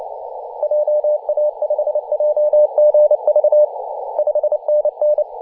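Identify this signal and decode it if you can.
Morse code, heard as a keyed tone, about 28 wpm, 600 Hz, with radio noise. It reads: JA5JGV 5NN
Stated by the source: DX station in Papua New Guinea on 21.040 MHz in CW.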